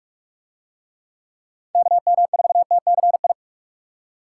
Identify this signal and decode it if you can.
Morse code, heard as a keyed tone, about 44 wpm, 700 Hz. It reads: KM4TCI